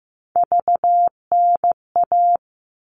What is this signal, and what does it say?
Morse code, heard as a keyed tone, about 15 wpm, 700 Hz. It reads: VNA
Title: All Cut Numbers - 3-Digit 15wpm